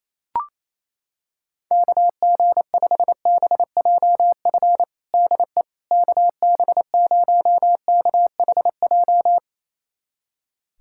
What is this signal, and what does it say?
Morse code, heard as a keyed tone, about 28 wpm, 700 Hz. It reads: KG5BJF DE KB0KHJ
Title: USA Callsign DE Callsign - 6 Characters - 28wpm